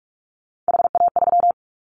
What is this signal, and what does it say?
Morse code, heard as a keyed tone, about 44 wpm, 700 Hz. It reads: HA3
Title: Callsign Prefixes for Top 25 DXCC - Repeated 3x - 45wpm